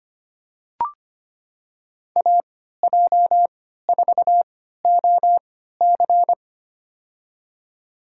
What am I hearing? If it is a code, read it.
Morse code, heard as a keyed tone, about 25 wpm, spaced as for 14 wpm, 700 Hz. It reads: AJ4OC
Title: USA Call Signs - 3x Character Spacing - 25wpm